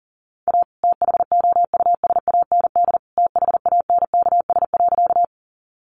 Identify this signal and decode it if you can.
Morse code, heard as a keyed tone, about 40 wpm, 700 Hz. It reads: A THOUSAND THANKS.
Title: Sentences from Top 600 Words 40wpm